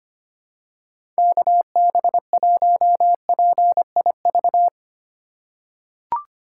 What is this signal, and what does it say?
Morse code, heard as a keyed tone, about 25 wpm, 700 Hz. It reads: KB1PIV
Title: USA Call Signs - 6 Characters - 3x Character Spacing - 25wpm